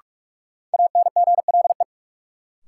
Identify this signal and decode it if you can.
Morse code, heard as a keyed tone, about 47 wpm, 700 Hz. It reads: ANGLE
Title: Sentences from Top 1500 Words - New Word Drill Alt - 45wpm